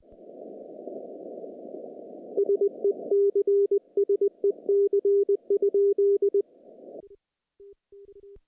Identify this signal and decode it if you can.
Morse code, heard as a keyed tone, about 20 wpm, 410 Hz. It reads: SEC SEC?
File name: contests / CW-SS-k5tr- -0000